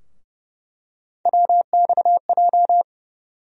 Morse code, heard as a keyed tone, about 30 wpm, 700 Hz. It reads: WXJ
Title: USA Call Sign Postfixes - 3 Letters 30wpm